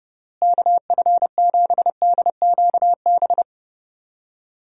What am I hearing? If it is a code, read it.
Morse code, heard as a keyed tone, about 30 wpm, 700 Hz. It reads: KF7DQB
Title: USA Call Signs - 6 Characters - Repeated 2x - 30wpm